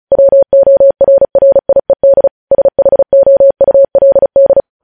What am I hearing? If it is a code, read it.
Morse code, heard as a keyed tone, about 36 wpm, 550 Hz. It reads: WORRIED SHOULD
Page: news headlines in morse code at 35 wpm